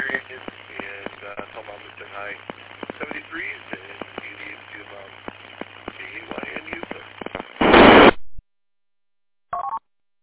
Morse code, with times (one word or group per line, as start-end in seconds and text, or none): none